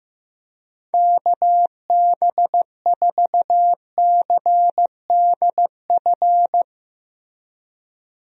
Morse code, heard as a keyed tone, about 15 wpm, 700 Hz. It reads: KB4CDF
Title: USA Call Signs - 6 Characters - Repeated 2x - 15wpm